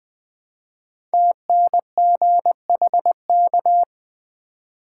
Morse code, heard as a keyed tone, about 20 wpm, 700 Hz. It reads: TNGHK
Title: Code Groups - 5 Letter:Numbers - 20wpm